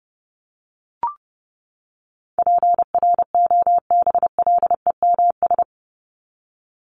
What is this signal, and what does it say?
Morse code, heard as a keyed tone, about 28 wpm, 700 Hz. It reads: PROBLEMS
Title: ES:S Suffix - Top 100 Words with Base Words 30wpm